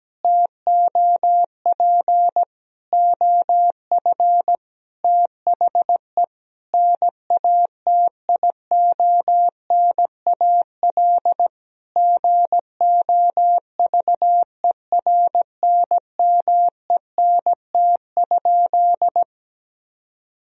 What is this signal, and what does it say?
Morse code, heard as a keyed tone, about 17 wpm, 700 Hz. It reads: TOP OF THE NATIONAL GOVERNMENT?
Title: Sentences from Top 1000 Words - Review Part 4 - 17wpm